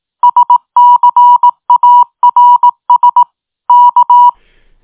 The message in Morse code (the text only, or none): SCARS K